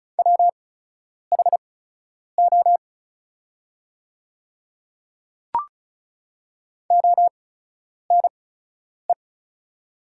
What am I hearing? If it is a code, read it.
Morse code, heard as a keyed tone, about 36 wpm, spaced as for 10 wpm, 700 Hz. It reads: WHO ONE
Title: 8x Character Spacing - Top 100 Words - 35wpm